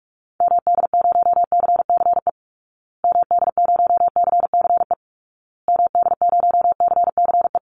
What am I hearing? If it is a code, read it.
Morse code, heard as a keyed tone, about 46 wpm, 700 Hz. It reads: MD0CCE MD0CCE MD0CCE
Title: Call Signs Repeated 2x - 45wpm